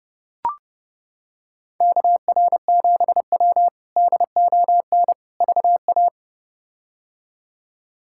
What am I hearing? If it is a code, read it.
Morse code, heard as a keyed tone, about 32 wpm, 700 Hz. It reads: KR7W DON VA
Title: K1USN SST QSO Practice - 30wpm